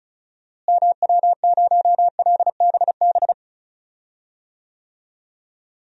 Morse code, heard as a keyed tone, about 34 wpm, 700 Hz. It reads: MW0LBB
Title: DXCC-100 Call Signs - 6 Characters - Repeated 2x - 35wpm